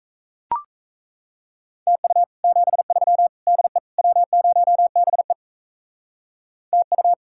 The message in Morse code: TU 73 DE W0BE TU

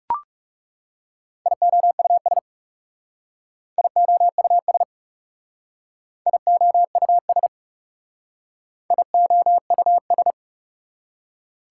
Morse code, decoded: IOUS IOUS IOUS IOUS

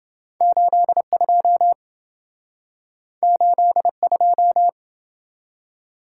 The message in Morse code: 82 82

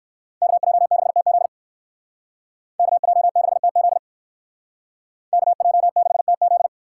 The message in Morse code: KJ6TZ KJ6TZ KJ6TZ

